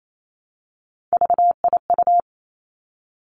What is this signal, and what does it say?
Morse code, heard as a keyed tone, about 28 wpm, 700 Hz. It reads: VIU
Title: Three Challenging Letter-Number Combinations 28wpm